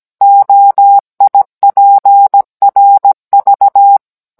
Morse code, heard as a keyed tone, about 17 wpm, 800 Hz. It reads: OIPRV